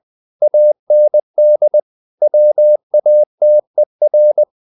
Morse code, heard as a keyed tone, about 20 wpm, 600 Hz. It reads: AND WATER
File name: the old man and the sea / the old man and the sea part 1 20wpm